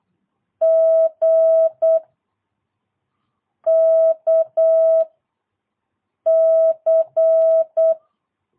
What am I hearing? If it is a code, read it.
Morse code, heard as a keyed tone, about 8 wpm, 650 Hz. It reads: G K C
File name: jh8cbh / 2022cw